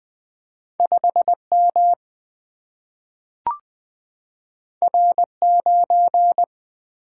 Morse code, heard as a keyed tone, about 20 wpm, 700 Hz. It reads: 5 M R 9
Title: Single-Letter Single-Number 20wpm